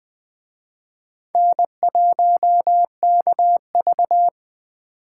Morse code, heard as a keyed tone, about 20 wpm, 700 Hz. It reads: N1KV